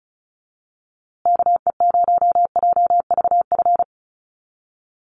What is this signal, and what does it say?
Morse code, heard as a keyed tone, about 34 wpm, 700 Hz. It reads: KE0JVF
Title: USA Call Signs Repeated 2x 35wpm